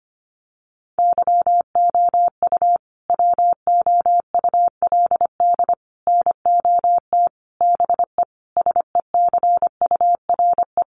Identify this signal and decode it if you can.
Morse code, heard as a keyed tone, about 25 wpm, 700 Hz. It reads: YOU WOULD NOT BE SECURE
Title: Sentences from Top 1200 Words 25wpm